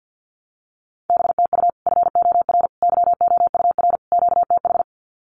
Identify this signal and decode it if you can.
Morse code, heard as a keyed tone, about 50 wpm, 700 Hz. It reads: BTU FOR YOUR QTH
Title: Common QSO Phrases 50wpm